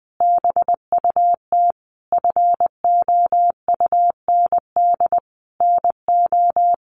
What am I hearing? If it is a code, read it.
Morse code, heard as a keyed tone, about 20 wpm, 700 Hz. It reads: BUT FOUND NO
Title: Sentences from Top 1400 Words 20wpm